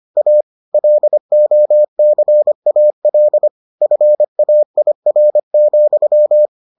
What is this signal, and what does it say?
Morse code, heard as a keyed tone, about 25 wpm, 600 Hz. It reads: A LOCAL FAIR,